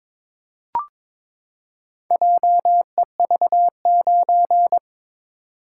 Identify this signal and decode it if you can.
Morse code, heard as a keyed tone, about 22 wpm, 700 Hz. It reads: JEV9
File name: practice / 22wpm / Four Letter-Number Combinations - 22wpm